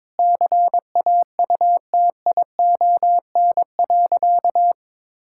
Morse code, heard as a keyed tone, about 22 wpm, 700 Hz. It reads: CAUTION.
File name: Sentences with 7-Word Phrases - 22wpm